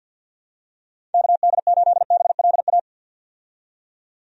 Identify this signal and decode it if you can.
Morse code, heard as a keyed tone, about 51 wpm, 700 Hz. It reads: KD8BLA
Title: USA Call Signs - 6 Characters - 50wpm